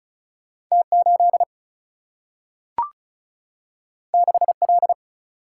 Morse code, heard as a keyed tone, about 36 wpm, 700 Hz. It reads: T 8 6 L
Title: Single-Letter Single-Number 35wpm